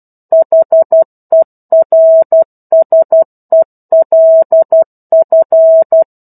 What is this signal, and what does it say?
Morse code, heard as a keyed tone, about 12 wpm, 650 Hz. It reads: HERSELF
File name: 01-Chapter-01 12 WPM